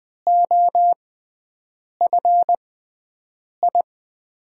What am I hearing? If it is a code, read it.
Morse code, heard as a keyed tone, about 20 wpm, spaced as for 7 wpm, 700 Hz. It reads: OFI
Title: USA Call Signs - 6 Characters - 6x Character Spacing - 20wpm